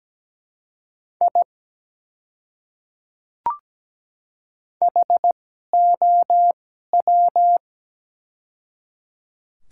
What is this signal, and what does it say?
Morse code, heard as a keyed tone, about 17 wpm, spaced as for 12 wpm, 700 Hz. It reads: I HOW
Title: 2x Character Spacing - Top 100 Words - 17wpm